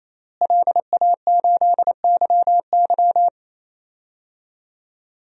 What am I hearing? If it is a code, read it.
Morse code, heard as a keyed tone, about 28 wpm, 700 Hz. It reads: LA8YY
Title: DXCC-100 Call Signs - 5 Characters - Repeated 3x - 28wpm